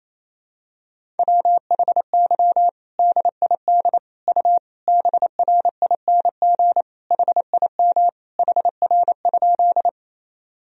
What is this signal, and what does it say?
Morse code, heard as a keyed tone, about 28 wpm, 700 Hz. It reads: WHY DID U BRING HIM HR?
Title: Sentences from Top 500 Words and Abbreviations 28wpm